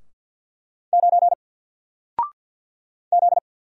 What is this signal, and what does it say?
Morse code, heard as a keyed tone, about 48 wpm, 700 Hz. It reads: 9 Z
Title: Single Character (Letter:Number:Punctuation:Prosign) Alt 50wpm